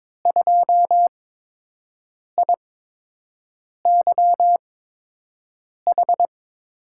Morse code, heard as a keyed tone, about 21 wpm, spaced as for 6 wpm, 700 Hz. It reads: 2IYH
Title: USA Call Signs - 8x Character Spacing - 22wpm